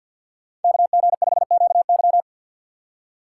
Morse code, heard as a keyed tone, about 52 wpm, 700 Hz. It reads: KG5QY
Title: USA Call Signs - 6x Character Spacing - 50wpm